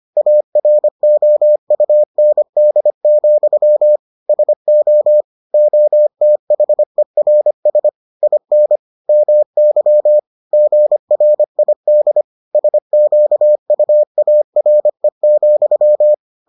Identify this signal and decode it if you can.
Morse code, heard as a keyed tone, about 25 wpm, 600 Hz. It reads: AROUND, SO OTHERS IN MY GRID SQUARE,